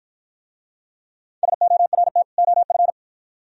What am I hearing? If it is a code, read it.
Morse code, heard as a keyed tone, about 46 wpm, 700 Hz. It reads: SORT OF